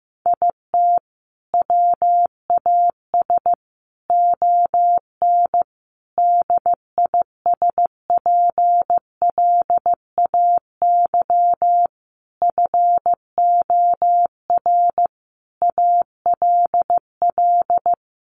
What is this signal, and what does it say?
Morse code, heard as a keyed tone, about 15 wpm, 700 Hz. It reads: IT WAS ON DISPLAY FOR ALL
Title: Sentences from Top 1200 Words 15wpm